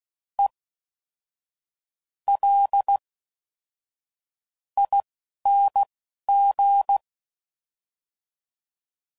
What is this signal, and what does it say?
Morse code, heard as a keyed tone, about 16 wpm, spaced as for 6 wpm, 800 Hz. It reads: ELING